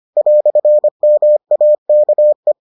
Morse code, heard as a keyed tone, about 25 wpm, 600 Hz. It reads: "MAKE